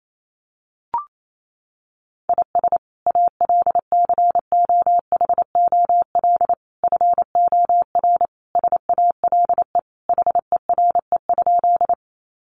Morse code, heard as a keyed tone, about 28 wpm, 700 Hz. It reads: IS ALCOHOL FOR SALE HERE?
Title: Sentences from Top 1400 Words 28wpm